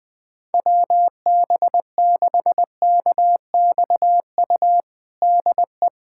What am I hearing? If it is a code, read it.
Morse code, heard as a keyed tone, about 21 wpm, 700 Hz. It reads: WB6KXU DE
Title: USA Callsign DE Callsign 20wpm